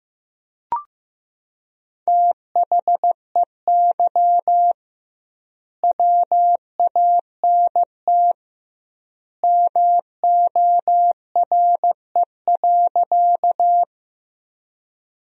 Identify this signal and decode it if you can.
Morse code, heard as a keyed tone, about 16 wpm, 700 Hz. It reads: THEY WANT MORE.
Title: 2x Word-Spacing - Sentences from Top 200 Words - 15wpm